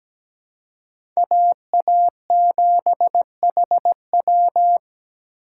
Morse code, heard as a keyed tone, about 17 wpm, 700 Hz. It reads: AA7HW
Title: USA Call Signs - 5 Characters - 4x Character Spacing - 17wpm